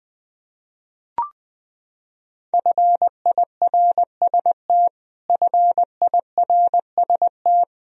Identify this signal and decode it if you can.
Morse code, heard as a keyed tone, about 20 wpm, 700 Hz. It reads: FIRST FIRST
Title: Top 100 Words Repeated 20wpm